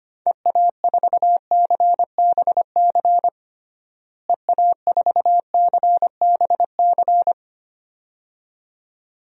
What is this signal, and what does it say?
Morse code, heard as a keyed tone, about 25 wpm, 700 Hz. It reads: EA4CBC EA4CBC
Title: DXCC-100 Call Signs - 6 Characters - Repeated 2x - 25wpm